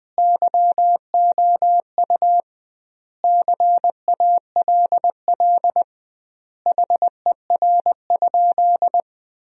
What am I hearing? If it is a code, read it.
Morse code, heard as a keyed tone, about 21 wpm, 700 Hz. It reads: YOU CALL HER?
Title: 2x Word-Spacing - Sentences from Top 100 Words - 20wpm